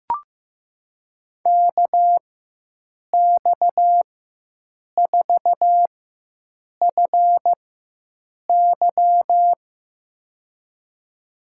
Morse code, heard as a keyed tone, about 15 wpm, spaced as for 7 wpm, 700 Hz. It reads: KX4FY
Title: USA Call Signs - 4x Character Spacing - 15wpm